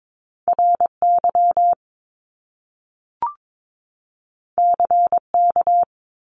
Morse code, heard as a keyed tone, about 22 wpm, 700 Hz. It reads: RY CK
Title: Top Two Letter Combinations in Words 22wpm